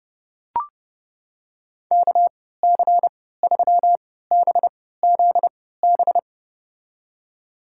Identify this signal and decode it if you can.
Morse code, heard as a keyed tone, about 31 wpm, spaced as for 17 wpm, 700 Hz. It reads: KC3BZB
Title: USA Call Signs - 6 Characters - 3x Character Spacing - 30wpm